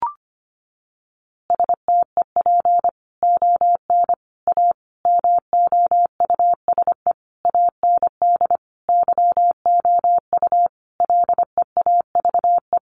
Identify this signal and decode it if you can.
Morse code, heard as a keyed tone, about 25 wpm, 700 Hz. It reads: STEP ON A MOUSE AND YOU LEAVE
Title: A Sound Of Thunder 25wpm